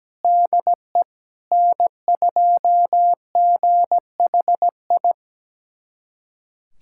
Morse code, heard as a keyed tone, about 17 wpm, 700 Hz. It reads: DE N2GHI
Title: USA Callsign DE Callsign 17wpm